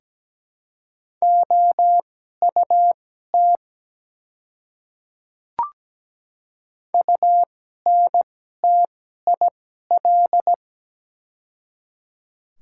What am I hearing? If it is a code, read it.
Morse code, heard as a keyed tone, about 17 wpm, spaced as for 12 wpm, 700 Hz. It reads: OUT UNTIL
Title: 2x Character Spacing - Top 300 Words - 17wpm